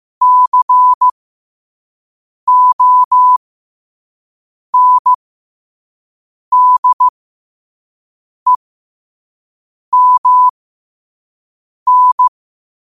Morse code, heard as a keyed tone, about 15 wpm, spaced as for 5 wpm, 1000 Hz. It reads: CONDEMN
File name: qotdfiles / 5wpm